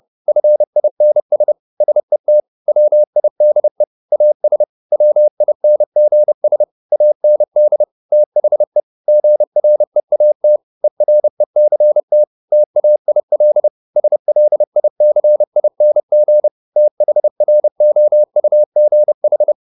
Morse code, heard as a keyed tone, about 30 wpm, 600 Hz. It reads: FINS SET WIDE AS WINGS AND THE GREAT ERECT TAIL SLICING THROUGH
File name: the old man and the sea / the old man and the sea 1 30wpm